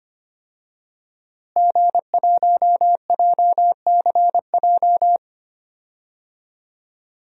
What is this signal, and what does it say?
Morse code, heard as a keyed tone, about 25 wpm, 700 Hz. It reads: G1JCJ
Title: DXCC-100 Call Signs - 5 Characters - Repeated 3x - 25wpm